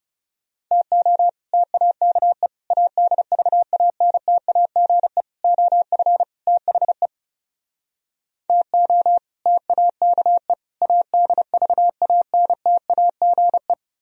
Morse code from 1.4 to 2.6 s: TAKE

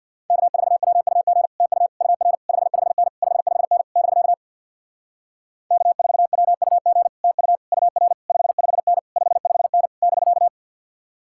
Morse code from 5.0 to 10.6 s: K4WUG TU UR 55N 55N <BK>

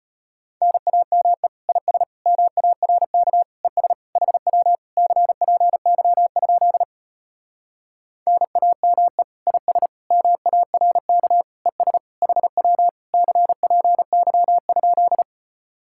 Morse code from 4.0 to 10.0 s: HW CPY? NAME IS

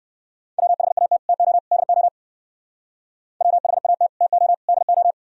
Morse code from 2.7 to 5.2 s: WHAT TO DO

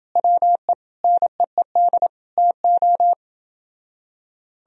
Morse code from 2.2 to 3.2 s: TO

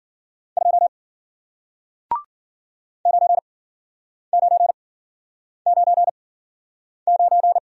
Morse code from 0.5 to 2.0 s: 2

